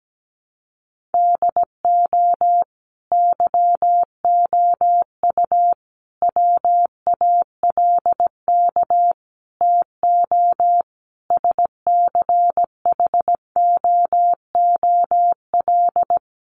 Morse code, heard as a keyed tone, about 17 wpm, 700 Hz. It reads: DO YOU WALK TO SCHOOL
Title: Sentences from Top 1400 Words 17wpm